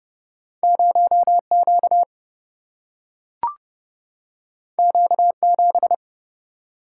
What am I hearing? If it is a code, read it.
Morse code, heard as a keyed tone, about 30 wpm, 700 Hz. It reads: 0 Q Q 7
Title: Single-Letter Single-Number 30wpm